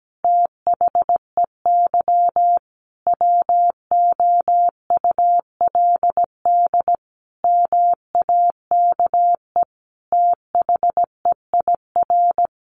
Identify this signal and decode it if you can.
Morse code, heard as a keyed tone, about 17 wpm, 700 Hz. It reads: THEY WOULD MAKE THEIR
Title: Sentences from Top 1000 Words 17wpm